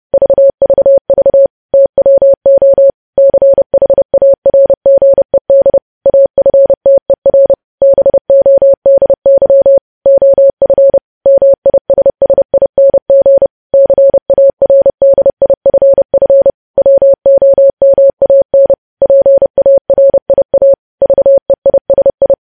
Morse code, heard as a keyed tone, about 30 wpm, 550 Hz. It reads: VVV TWO CHARGED AFTER BODY OF MISSING CARDIFF WOMAN PARIA VEISI
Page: news headlines in morse code at 30 wpm